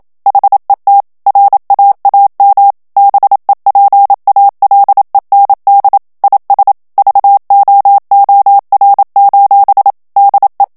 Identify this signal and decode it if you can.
Morse code, heard as a keyed tone, about 27 wpm, 800 Hz. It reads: HET RAAM BEPALEND IS VOOR: DE
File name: HOLLANDAIS texte clair 1 vitesse 26 mots minute